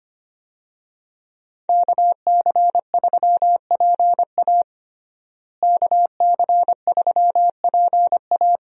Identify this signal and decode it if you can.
Morse code, heard as a keyed tone, about 25 wpm, 700 Hz. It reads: KC3PA KC3PA
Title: USA Call Signs - 5 Characters - Repeated 3x - 25wpm